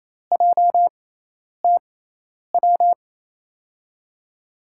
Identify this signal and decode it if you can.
Morse code, heard as a keyed tone, about 28 wpm, spaced as for 10 wpm, 700 Hz. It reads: JTW
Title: USA Call Signs - 5 Characters - 6x Character Spacing - 28wpm